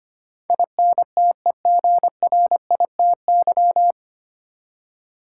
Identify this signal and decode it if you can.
Morse code, heard as a keyed tone, about 25 wpm, 700 Hz. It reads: INTEGRITY
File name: Nine Letter Words - 25wpm